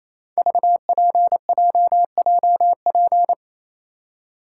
Morse code, heard as a keyed tone, about 27 wpm, 700 Hz. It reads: VPJJP
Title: Code Groups - 5 Letters - 28wpm